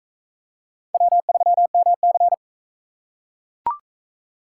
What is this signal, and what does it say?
Morse code, heard as a keyed tone, about 41 wpm, 700 Hz. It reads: W3MC